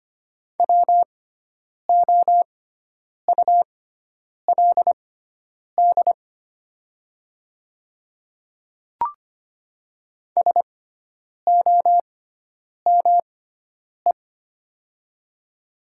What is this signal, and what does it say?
Morse code, heard as a keyed tone, about 25 wpm, spaced as for 9 wpm, 700 Hz. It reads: WOULD SOME
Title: 6x Character Spacing - Top 100 Words - 25wpm